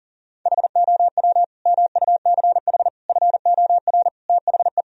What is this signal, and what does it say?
Morse code, heard as a keyed tone, about 40 wpm, 700 Hz. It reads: HOW MUCH FOR THE
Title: Sentences from Top 1000 Words - Review Part 3 - 40wpm